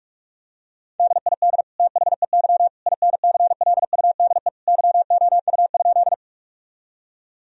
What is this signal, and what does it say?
Morse code, heard as a keyed tone, about 44 wpm, 700 Hz. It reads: DID THEY INCLUDE YOU?
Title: Sentences from Top 1000 Words - Review Part 3 - 45wpm